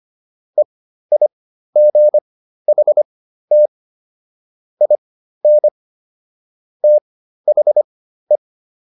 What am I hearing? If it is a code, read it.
Morse code, heard as a keyed tone, about 25 wpm, spaced as for 13 wpm, 600 Hz. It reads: EIGHT IN THE